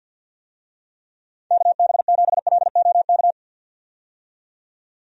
Morse code, heard as a keyed tone, about 51 wpm, 700 Hz. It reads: KB7LOK
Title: USA Call Signs - 6 Characters - 50wpm